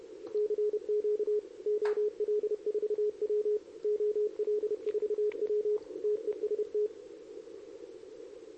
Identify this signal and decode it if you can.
Morse code, heard as a keyed tone, about 29 wpm, 420 Hz. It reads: CQ OL4W OL4W TEST
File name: CQWWCW2023 / RUN